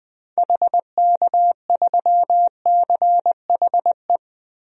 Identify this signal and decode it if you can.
Morse code, heard as a keyed tone, about 21 wpm, 700 Hz. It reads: HK3CHE
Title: DXCC-100 Call Signs - 6 Characters - 20wpm